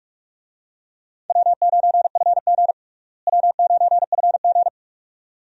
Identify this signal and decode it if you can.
Morse code, heard as a keyed tone, about 42 wpm, 700 Hz. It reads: W9FG W9FG
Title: USA Call Signs - 4 Characters - Repeated 2x - 45wpm